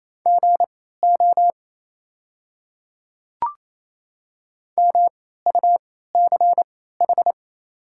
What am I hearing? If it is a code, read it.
Morse code, heard as a keyed tone, about 27 wpm, spaced as for 16 wpm, 700 Hz. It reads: GO MUCH
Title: 3x Character Spacing - Top 200 Words - 28wpm